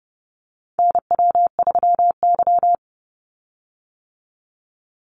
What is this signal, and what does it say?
Morse code, heard as a keyed tone, about 30 wpm, 700 Hz. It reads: NW3Y
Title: Call Signs - 30wpm